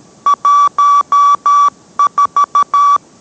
Morse code, heard as a keyed tone, about 13 wpm, 1150 Hz. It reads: 14